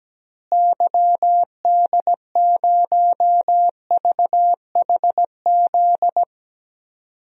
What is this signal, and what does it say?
Morse code, heard as a keyed tone, about 17 wpm, 700 Hz. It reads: YD0VHZ
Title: DXCC-100 Call Signs - 6 Characters - Repeated 2x - 17wpm